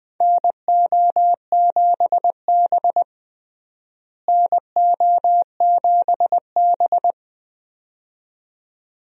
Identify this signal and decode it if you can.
Morse code, heard as a keyed tone, about 20 wpm, 700 Hz. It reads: NO7B NO7B
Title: USA Call Signs - 4 Characters - Repeated 2x - 20wpm